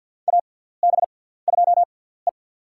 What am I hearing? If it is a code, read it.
Morse code, heard as a keyed tone, about 50 wpm, spaced as for 17 wpm, 700 Hz. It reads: AB2E
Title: USA Call Signs - 4 Characters - 6x Character Spacing - 50wpm